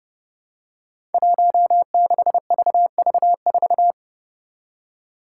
Morse code, heard as a keyed tone, about 30 wpm, 700 Hz. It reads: 16VV4
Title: Five Challenging Letter-Number Combinations 30wpm